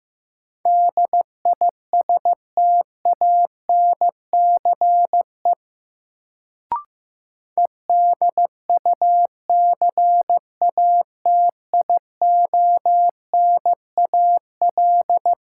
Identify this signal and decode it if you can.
Morse code, heard as a keyed tone, about 16 wpm, 700 Hz. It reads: DISTANCE EDUCATIONAL
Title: Sentences from Top 1200 Words - New Word Drill - 15wpm